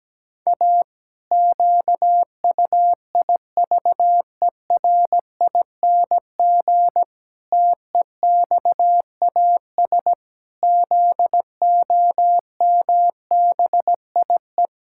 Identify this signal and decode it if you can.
Morse code, heard as a keyed tone, about 17 wpm, 700 Hz. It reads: A QUIVERING TEXAS ZOMBIE
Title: Pangram Challenge v2 - 17wpm